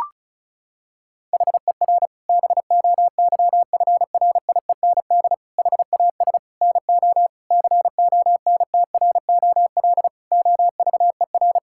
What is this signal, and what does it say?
Morse code, heard as a keyed tone, about 34 wpm, 700 Hz. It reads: HER BOYFRIEND HAS NO CONTROL OVER
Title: Sentences from Top 1000 Words - Review Part 5 - 35wpm